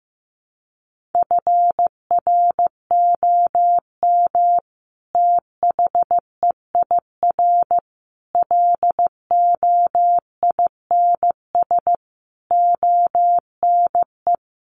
Morse code, heard as a keyed tone, about 15 wpm, 700 Hz. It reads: FROM THEIR LOINS ONE
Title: A Sound Of Thunder 15wpm